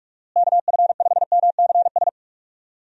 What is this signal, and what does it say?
Morse code, heard as a keyed tone, about 41 wpm, 700 Hz. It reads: KF5MCS
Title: USA Call Signs - 6 Characters - Repeated 2x - 45wpm